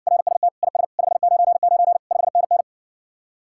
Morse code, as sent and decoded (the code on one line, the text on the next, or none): .-. ... - / .. ... / ..... ----. ----. / ..... -. -.
RST IS 599 5NN